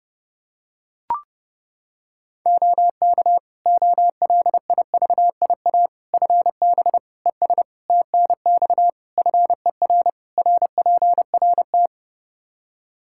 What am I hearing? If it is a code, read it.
Morse code, heard as a keyed tone, about 30 wpm, 700 Hz. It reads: OK OLIVIA FB ES TNX FER RPRT
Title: Practice QSO - Exchange 2 - Info - 30wpm